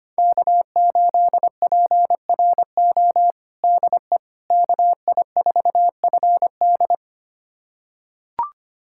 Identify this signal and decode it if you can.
Morse code, heard as a keyed tone, about 25 wpm, 700 Hz. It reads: K8PRO DE KI4FD
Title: USA Callsign DE Callsign - 5 Characters - 25wpm